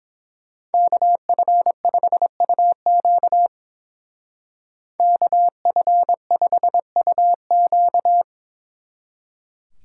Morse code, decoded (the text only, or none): KF5UQ KF5UQ